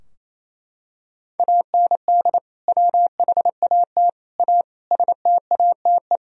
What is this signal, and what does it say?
Morse code, heard as a keyed tone, about 28 wpm, 700 Hz. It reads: AND WHAT A STATE
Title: Sentences from Top 1000 Words - Review Part 2 - 28wpm